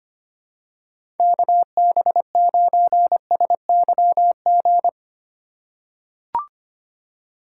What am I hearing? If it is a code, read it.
Morse code, heard as a keyed tone, about 25 wpm, 700 Hz. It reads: KB9SYG
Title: USA Call Signs - 6 Characters - 3x Character Spacing - 25wpm